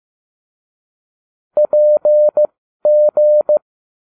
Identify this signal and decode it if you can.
Morse code, heard as a keyed tone, about 15 wpm, spaced as for 12 wpm, 600 Hz. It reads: PG